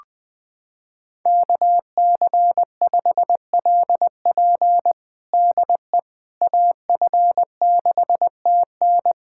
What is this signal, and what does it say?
Morse code, heard as a keyed tone, about 20 wpm, 700 Hz. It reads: KC5LP DE AF6TN
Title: USA Callsign DE Callsign - 5 Characters - 20wpm